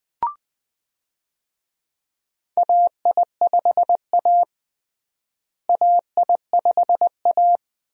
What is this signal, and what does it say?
Morse code, heard as a keyed tone, about 20 wpm, 700 Hz. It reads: AI5A AI5A
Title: USA Call Signs - 4 Characters - Repeated 2x - 20wpm